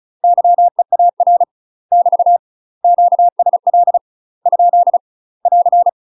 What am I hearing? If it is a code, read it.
Morse code, heard as a keyed tone, about 34 wpm, 700 Hz. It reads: YEAR <BT> QSL ? <AR>
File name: UK MASTER QSOs TRAINING / 35WPM / UK MASTER QSOs 0043